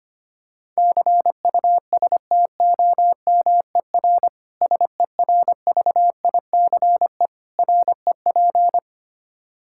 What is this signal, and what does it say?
Morse code, heard as a keyed tone, about 25 wpm, 700 Hz. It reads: CUSTOMER SERVICE REP
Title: Practice QSO - Exchange 3 - More Info - 25wpm